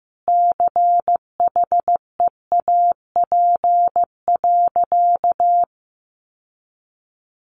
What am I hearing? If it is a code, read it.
Morse code, heard as a keyed tone, about 14 wpm, 700 Hz. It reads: CHEAP.
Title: Sentences from Top 1200 Words 15wpm